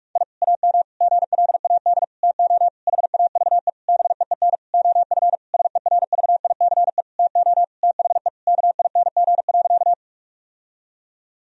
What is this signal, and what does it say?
Morse code, heard as a keyed tone, about 46 wpm, 700 Hz. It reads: I AM GLAD TO HAVE BEEN OF SERVICE TO THE KING.
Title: Sentences from Top 1000 Words - Review Part 4 - 45wpm